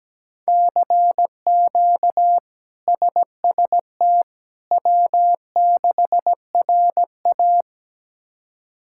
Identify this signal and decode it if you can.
Morse code, heard as a keyed tone, about 17 wpm, 700 Hz. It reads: CQ SST W6RA
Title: K1USN SST QSO Practice - 17wpm